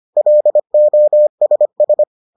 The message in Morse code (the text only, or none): LOSS